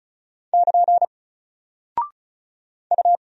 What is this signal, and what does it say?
Morse code, heard as a keyed tone, about 36 wpm, 700 Hz. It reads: <KN> U
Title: Single Character (Letter:Number:Punctuation:Prosign) Alt 35wpm